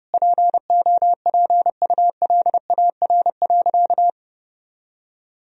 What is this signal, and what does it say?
Morse code, heard as a keyed tone, about 30 wpm, 700 Hz. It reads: POPULAR.
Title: Sentences from Top 1000 Words - Review Part 3 - 30wpm